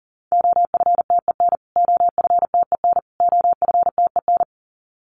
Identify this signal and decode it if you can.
Morse code, heard as a keyed tone, about 37 wpm, 700 Hz. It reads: OFTEN OFTEN OFTEN
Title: Top 500 Words Repeated 40wpm